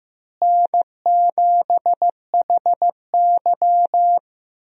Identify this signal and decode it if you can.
Morse code, heard as a keyed tone, about 15 wpm, 700 Hz. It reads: N7HY
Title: USA Call Signs - 4 Characters - Repeated 2x - 15wpm